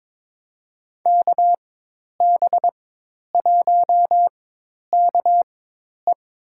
Morse code, heard as a keyed tone, about 22 wpm, spaced as for 10 wpm, 700 Hz. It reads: KB1KE
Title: USA Call Signs - 5 Characters - 4x Character Spacing - 22wpm